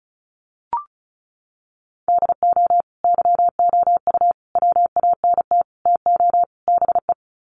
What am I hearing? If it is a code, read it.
Morse code, heard as a keyed tone, about 34 wpm, 700 Hz. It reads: DO YOU WANT TO BE
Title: Sets of 5 Words Encore2 35wpm